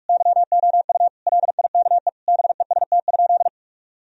Morse code, heard as a keyed tone, about 42 wpm, 700 Hz. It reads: YOU LIKE BEST?
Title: Sentences from Top 1000 Words - Review Part 1 - 45wpm